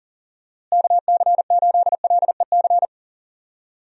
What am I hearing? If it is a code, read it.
Morse code, heard as a keyed tone, about 39 wpm, 700 Hz. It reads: KC8LEC
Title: USA Call Signs - 6 Characters - Repeated 3x - 40wpm